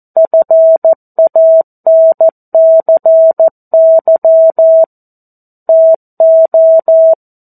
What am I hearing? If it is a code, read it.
Morse code, heard as a keyed tone, about 14 wpm, 650 Hz. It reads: FANCY TO
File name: 02-Chapter-02 14 WPM